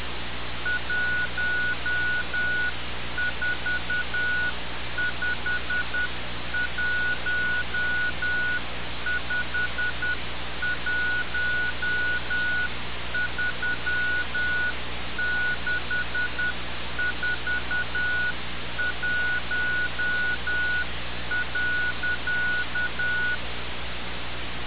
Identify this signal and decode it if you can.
Morse code, heard as a keyed tone, about 10 wpm, 1460 Hz. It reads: 1451513641.
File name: radiocap / REC z